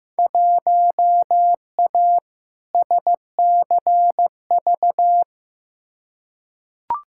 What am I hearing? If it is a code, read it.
Morse code, heard as a keyed tone, about 15 wpm, 700 Hz. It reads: 1A SCV
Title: Field Day Exchanges - 15wpm